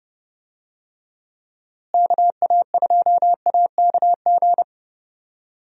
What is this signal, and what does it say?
Morse code, heard as a keyed tone, about 29 wpm, 700 Hz. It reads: KA2AKG
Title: USA Call Signs - 6 Characters - Repeated 3x - 30wpm